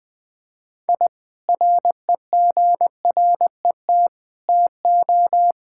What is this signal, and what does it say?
Morse code, heard as a keyed tone, about 20 wpm, 700 Hz. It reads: I REGRET TO
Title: Sentences with 3-Word Phrases - 20wpm